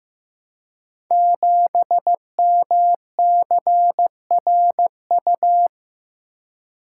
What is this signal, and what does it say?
Morse code, heard as a keyed tone, about 15 wpm, 700 Hz. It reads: 7MCRU
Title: Code Groups - 5 Letter:Numbers - 15wpm